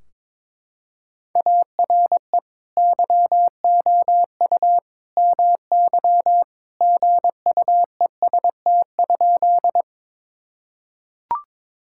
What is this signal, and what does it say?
Morse code, heard as a keyed tone, about 21 wpm, 700 Hz. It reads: ARE YOU MY GUEST?
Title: Sentences from Top 1600 Words 22wpm